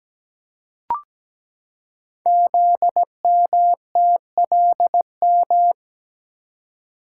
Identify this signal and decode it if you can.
Morse code, heard as a keyed tone, about 17 wpm, 700 Hz. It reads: ZMTLM
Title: Code Groups - 5 Letters - 17wpm